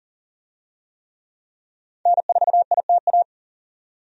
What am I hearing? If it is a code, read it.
Morse code, heard as a keyed tone, about 40 wpm, 700 Hz. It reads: N4ITA
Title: USA Call Signs - 5 Characters - 40wpm